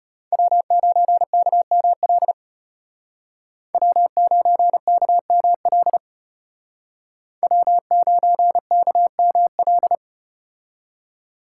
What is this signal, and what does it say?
Morse code, heard as a keyed tone, about 34 wpm, 700 Hz. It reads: W9KML W9KML W9KML